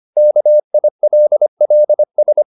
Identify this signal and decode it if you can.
Morse code, heard as a keyed tone, about 25 wpm, 600 Hz. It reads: KILLS
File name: the old man and the sea part 1 25wpm